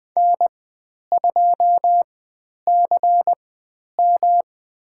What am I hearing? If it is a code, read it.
Morse code, heard as a keyed tone, about 20 wpm, spaced as for 10 wpm, 700 Hz. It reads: N2CM
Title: 109 Lesson 18g - Letter m - Call Signs 20wpm